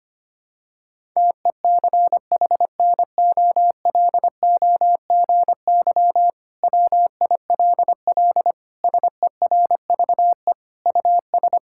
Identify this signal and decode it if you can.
Morse code, heard as a keyed tone, about 25 wpm, 700 Hz. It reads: TECHNOLOGY WILL SERVE US
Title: Sentences from Top 1000 Words - Review Part 5 - 25wpm